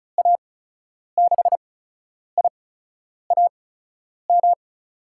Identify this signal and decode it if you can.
Morse code, heard as a keyed tone, about 36 wpm, spaced as for 10 wpm, 700 Hz. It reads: A6IAM
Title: USA Call Signs - 6 Characters - 8x Character Spacing - 35wpm